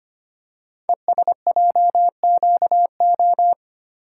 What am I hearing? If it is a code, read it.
Morse code, heard as a keyed tone, about 25 wpm, 700 Hz. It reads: ESJQO
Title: Code Groups - 5 Letters - 25wpm